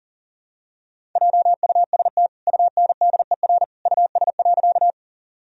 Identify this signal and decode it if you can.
Morse code, heard as a keyed tone, about 40 wpm, 700 Hz. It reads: JUST UNDER US.
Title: Sentences from Top 1000 Words - Review Part 1 - 40wpm